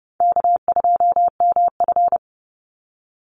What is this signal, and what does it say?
Morse code, heard as a keyed tone, about 29 wpm, 700 Hz. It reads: K2MF
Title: Easy Call Signs Repeated 3x - 30wpm